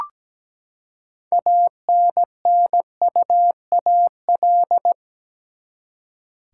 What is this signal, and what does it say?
Morse code, heard as a keyed tone, about 17 wpm, 700 Hz. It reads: ANNUAL
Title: Six Letter Words - 17wpm